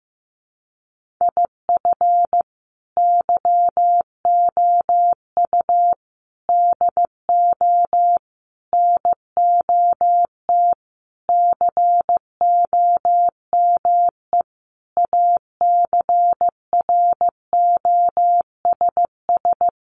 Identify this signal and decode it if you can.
Morse code, heard as a keyed tone, about 15 wpm, 700 Hz. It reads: IF YOU DO NOT COME ACROSS
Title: Sentences from Top 800 Words 15wpm